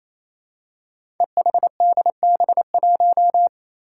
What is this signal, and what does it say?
Morse code, heard as a keyed tone, about 28 wpm, 700 Hz. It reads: EHDB1
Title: Five Challenging Letter-Number Combinations 28wpm